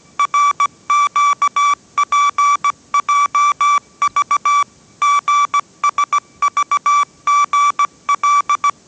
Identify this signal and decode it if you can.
Morse code, heard as a keyed tone, about 17 wpm, 1160 Hz. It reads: RQPJV GSVGL
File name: 18wpm